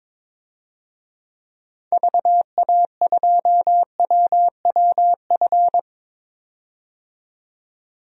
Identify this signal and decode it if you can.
Morse code, heard as a keyed tone, about 22 wpm, 700 Hz. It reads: VA2WWF
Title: DXCC-100 Call Signs - 6 Characters - 22wpm